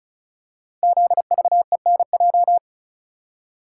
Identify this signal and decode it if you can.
Morse code, heard as a keyed tone, about 36 wpm, 700 Hz. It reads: ZVENJ
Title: Code Groups - 5 Letters - 35wpm